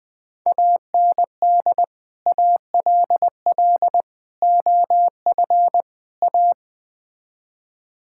Morse code, heard as a keyed tone, about 20 wpm, 700 Hz. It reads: AND ALL OF A
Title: Build A Phrase - 5-Word Phrases - 20wpm